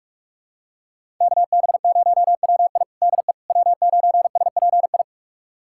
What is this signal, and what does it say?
Morse code, heard as a keyed tone, about 42 wpm, 700 Hz. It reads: KB0WI DE W9SPI
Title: USA Callsign DE Callsign - 5 Characters - 45wpm